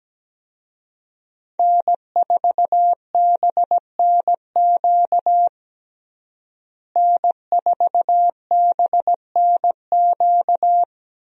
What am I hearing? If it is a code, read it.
Morse code, heard as a keyed tone, about 17 wpm, 700 Hz. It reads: N4BNQ N4BNQ
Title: USA Call Signs - 5 Characters - Repeated 2x - 17wpm